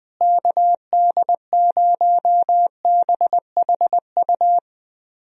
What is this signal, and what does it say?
Morse code, heard as a keyed tone, about 21 wpm, 700 Hz. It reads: KD0BHU
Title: USA Call Signs - 6 Characters - Repeated 2x - 20wpm